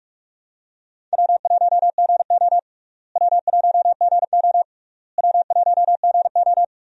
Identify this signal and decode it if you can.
Morse code, heard as a keyed tone, about 42 wpm, 700 Hz. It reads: W1GO W1GO W1GO
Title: USA Call Signs - 4 Characters - Repeated 3x - 45wpm